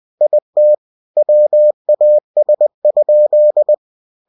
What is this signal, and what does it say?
Morse code, heard as a keyed tone, about 20 wpm, 600 Hz. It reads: IT WAS?
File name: cw-books / winnie-the-pooh / winnie-20wpm-page-0015